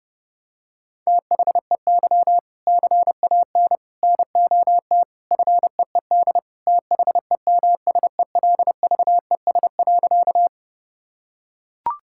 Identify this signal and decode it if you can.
Morse code, heard as a keyed tone, about 32 wpm, 700 Hz. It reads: THEY CAN NOT FEED THEMSELVES.
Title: Sentences from Top 1600 Words 30wpm